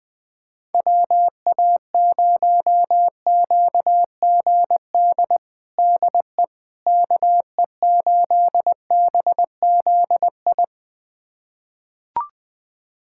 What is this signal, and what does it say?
Morse code, heard as a keyed tone, about 20 wpm, 700 Hz. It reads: WA0QGD DE KE8BZI
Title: USA Callsign DE Callsign 20wpm